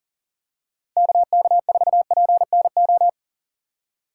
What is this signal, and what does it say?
Morse code, heard as a keyed tone, about 40 wpm, 700 Hz. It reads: KK4PNO
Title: USA Call Signs - 6 Characters - 3x Character Spacing - 40wpm